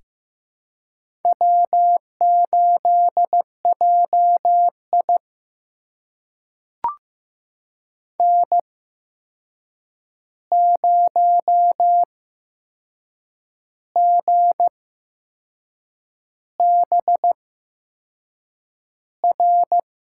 Morse code, heard as a keyed tone, about 15 wpm, spaced as for 6 wpm, 700 Hz. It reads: W8JI N0GBR
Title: USA Call Signs - 8x Character Spacing - 15wpm